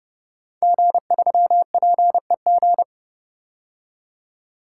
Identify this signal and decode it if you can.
Morse code, heard as a keyed tone, about 30 wpm, 700 Hz. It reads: G3PEG
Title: Code Groups - 5 Letter:Numbers - 30wpm